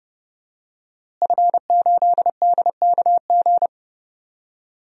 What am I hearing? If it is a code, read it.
Morse code, heard as a keyed tone, about 30 wpm, 700 Hz. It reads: F8DKG
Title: DXCC-100 Call Signs - 5 Characters - Repeated 2x - 30wpm